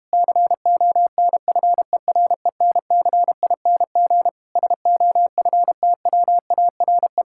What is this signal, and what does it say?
Morse code, heard as a keyed tone, about 32 wpm, 700 Hz. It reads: CONFERENCING SOFTWARE